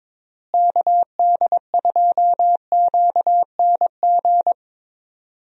Morse code, heard as a keyed tone, about 22 wpm, 700 Hz. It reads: KD2QNG
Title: USA Call Signs - 6 Characters - Repeated 3x - 22wpm